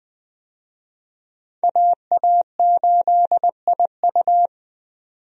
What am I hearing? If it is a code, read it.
Morse code, heard as a keyed tone, about 21 wpm, 700 Hz. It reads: AA8IU
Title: USA Call Signs - 5 Characters - 20wpm